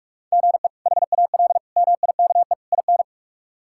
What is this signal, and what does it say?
Morse code, heard as a keyed tone, about 47 wpm, 700 Hz. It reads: GE HAL MIKE IN
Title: K1USN SST QSO Practice - 45wpm